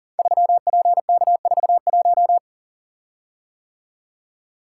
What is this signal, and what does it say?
Morse code, heard as a keyed tone, about 39 wpm, 700 Hz. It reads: 3PK41
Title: Code Groups - 5 Letter:Numbers - 40wpm